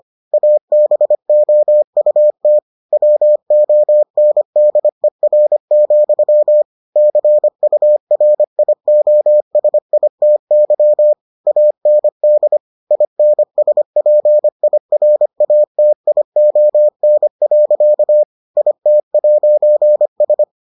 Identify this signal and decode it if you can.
Morse code, heard as a keyed tone, about 25 wpm, 600 Hz. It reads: ABOUT WONDER, CURIOSITY AND INSPIRATION. IT'S